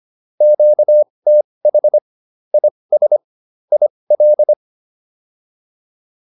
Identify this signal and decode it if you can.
Morse code, heard as a keyed tone, about 25 wpm, spaced as for 20 wpm, 600 Hz. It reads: QTH IS IL